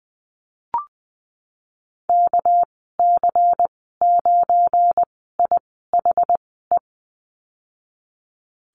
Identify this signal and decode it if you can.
Morse code, heard as a keyed tone, about 20 wpm, spaced as for 15 wpm, 700 Hz. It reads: KC9IHE
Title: 213 Lesson 34g - Number 8 - Call Signs 20wpm